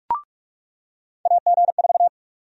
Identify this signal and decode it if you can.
Morse code, heard as a keyed tone, about 42 wpm, 700 Hz. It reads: AG4
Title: USA Call Sign Prefixes - 45wpm